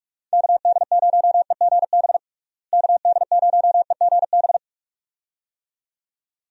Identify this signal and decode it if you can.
Morse code, heard as a keyed tone, about 46 wpm, 700 Hz. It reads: KD0EGB KD0EGB
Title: USA Call Signs - 6 Characters - Repeated 2x - 45wpm